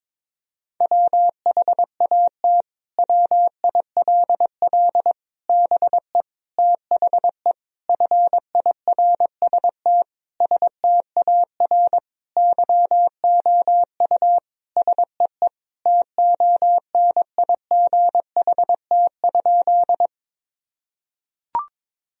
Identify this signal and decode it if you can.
Morse code, heard as a keyed tone, about 22 wpm, 700 Hz. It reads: WHAT WILL BE THE FIRST STAR YOU SEE TONIGHT?
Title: Sentences from Top 1000 Words - Review Part 2 - 22wpm